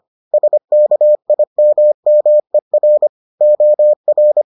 Morse code, heard as a keyed tone, about 25 wpm, 600 Hz. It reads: SKIMMER OR